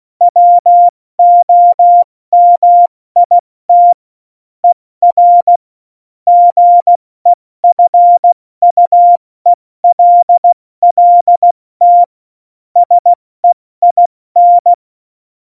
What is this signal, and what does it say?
Morse code, heard as a keyed tone, about 16 wpm, 700 Hz. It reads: WOMIT ER GEFUELLT SEIN
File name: Morsekurs / Witze / Witz 17. November 2020, 70 F80